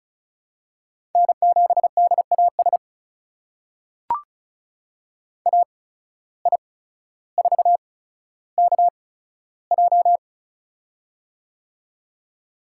Morse code, heard as a keyed tone, about 36 wpm, spaced as for 15 wpm, 700 Hz. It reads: N7DAS AI4KJ
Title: USA Call Signs - 5 Characters - 8x Character Spacing - 35wpm